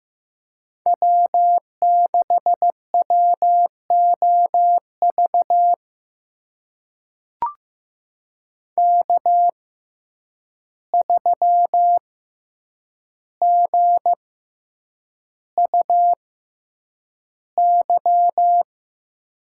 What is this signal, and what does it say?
Morse code, heard as a keyed tone, about 15 wpm, spaced as for 8 wpm, 700 Hz. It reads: W6WOV K3GUY
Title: USA Call Signs - 5 Characters - 6x Character Spacing - 15wpm